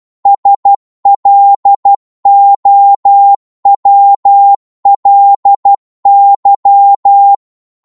Morse code, heard as a keyed tone, about 12 wpm, 800 Hz. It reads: SLOWLY
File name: Morse / qso-12wpm